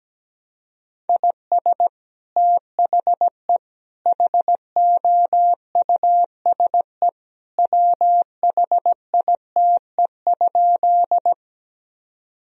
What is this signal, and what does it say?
Morse code, heard as a keyed tone, about 17 wpm, 700 Hz. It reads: IS THE HOUSE WHITE?
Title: Sentences from Top 1000 Words - Review Part 1 - 17wpm